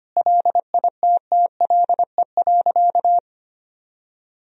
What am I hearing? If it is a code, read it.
Morse code, heard as a keyed tone, about 25 wpm, 700 Hz. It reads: LITTLE.
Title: Sentences from Top 1000 Words - Review Part 1 - 25wpm